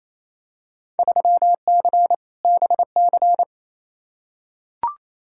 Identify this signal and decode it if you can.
Morse code, heard as a keyed tone, about 28 wpm, 700 Hz. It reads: 3C BC
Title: Field Day Exchanges Encore - 28wpm